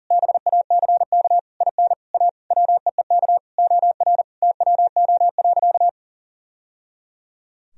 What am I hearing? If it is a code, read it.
Morse code, heard as a keyed tone, about 40 wpm, 700 Hz. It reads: BACK IN A WEEK OR TWO.
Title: Sentences from Top 1000 Words - Review Part 2 - 40wpm